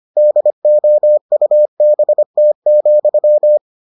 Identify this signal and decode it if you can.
Morse code, heard as a keyed tone, about 25 wpm, 600 Hz. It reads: DOUBT,